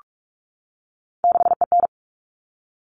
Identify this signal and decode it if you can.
Morse code, heard as a keyed tone, about 44 wpm, 700 Hz. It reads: 6EN
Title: All Cut Numbers - 3-Digit 45wpm